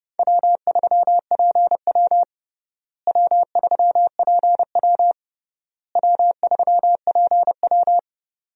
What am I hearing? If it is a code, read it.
Morse code, heard as a keyed tone, about 32 wpm, 700 Hz. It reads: W3PW W3PW W3PW
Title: USA Call Signs - 4 Characters - Repeated 3x - 30wpm